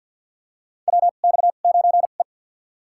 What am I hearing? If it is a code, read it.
Morse code, heard as a keyed tone, about 46 wpm, spaced as for 35 wpm, 700 Hz. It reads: WX9E
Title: USA Call Signs - 4 Characters - 2x Character Spacing - 50wpm